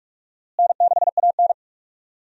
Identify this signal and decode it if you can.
Morse code, heard as a keyed tone, about 46 wpm, 700 Hz. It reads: N6AN
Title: USA Call Signs - 4 Characters - Repeated 3x - 45wpm